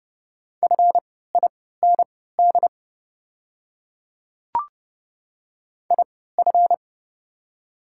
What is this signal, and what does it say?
Morse code, heard as a keyed tone, about 30 wpm, spaced as for 17 wpm, 700 Hz. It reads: FIND IF
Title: 3x Character Spacing - Top 100 Words - 30wpm